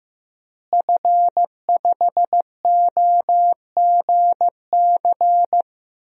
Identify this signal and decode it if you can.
Morse code, heard as a keyed tone, about 15 wpm, 700 Hz. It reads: F5OGC
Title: DXCC-100 Call Signs - 5 Characters - Repeated 2x - 15wpm